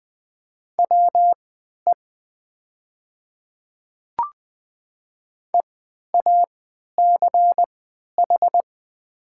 Morse code, heard as a keyed tone, about 20 wpm, spaced as for 11 wpm, 700 Hz. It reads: WE EACH
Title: 3x Character Spacing - Top 200 Words - 20wpm